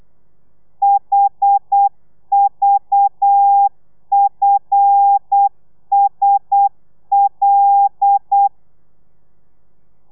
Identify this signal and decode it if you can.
Morse code, heard as a keyed tone, about 8 wpm, 800 Hz. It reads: HVFSL
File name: telegrafia / i2tfj / Lez5 35